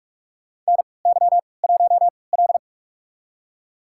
Morse code, heard as a keyed tone, about 45 wpm, spaced as for 25 wpm, 700 Hz. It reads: NY1L
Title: USA Call Signs - 4 Characters - 3x Character Spacing - 45wpm